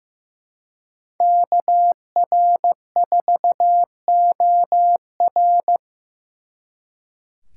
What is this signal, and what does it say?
Morse code, heard as a keyed tone, about 18 wpm, 700 Hz. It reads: KR4OR
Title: Speed Racing - USA Call Signs 15wpm